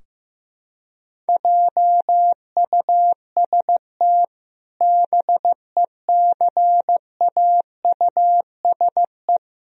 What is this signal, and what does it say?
Morse code, heard as a keyed tone, about 15 wpm, 700 Hz. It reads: JUST BECAUSE
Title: Sentences from Top 1000 Words - Review Part 3 - 15wpm